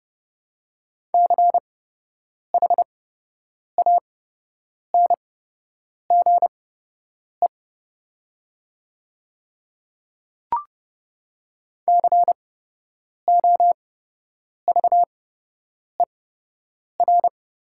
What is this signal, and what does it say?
Morse code, heard as a keyed tone, about 29 wpm, spaced as for 8 wpm, 700 Hz. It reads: CHANGE COVER
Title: 8x Character Spacing - Top 300 Words - 30wpm